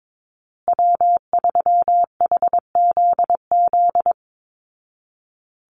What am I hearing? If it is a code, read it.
Morse code, heard as a keyed tone, about 22 wpm, 700 Hz. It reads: W3HZZ
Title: Call Signs Repeated 3x - 22wpm